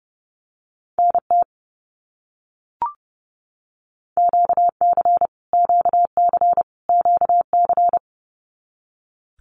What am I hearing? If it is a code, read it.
Morse code, heard as a keyed tone, about 30 wpm, 700 Hz. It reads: NT QC QC QC
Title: Canadian Provinces and Territories - Repeated 3x - 30wpm